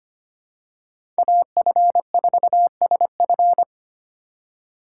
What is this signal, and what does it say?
Morse code, heard as a keyed tone, about 25 wpm, 700 Hz. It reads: AF4SF
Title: USA Call Signs - 5 Characters - 4x Character Spacing - 25wpm